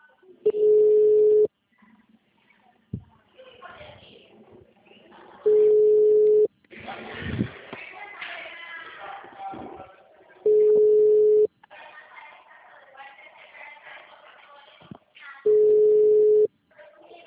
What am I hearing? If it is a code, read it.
Ringback tone (Europe, 425 Hz), with speech.